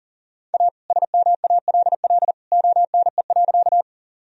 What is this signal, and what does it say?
Morse code, heard as a keyed tone, about 40 wpm, 700 Hz. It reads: A SMALL ONE.
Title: Sentences from Top 1000 Words - Review Part 1 - 40wpm